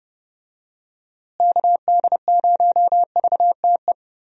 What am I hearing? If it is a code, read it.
Morse code, heard as a keyed tone, about 30 wpm, 700 Hz. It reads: KD0VTE
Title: USA Call Signs - 6 Characters - Repeated 3x - 30wpm